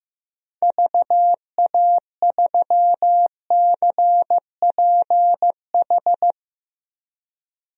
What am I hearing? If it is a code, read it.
Morse code, heard as a keyed tone, about 16 wpm, 700 Hz. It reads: VA3CPH
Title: DXCC-100 Call Signs - 6 Characters - Repeated 3x - 15wpm